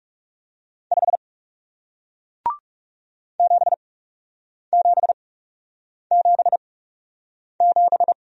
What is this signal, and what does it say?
Morse code, heard as a keyed tone, about 40 wpm, spaced as for 8 wpm, 700 Hz. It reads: 5 7777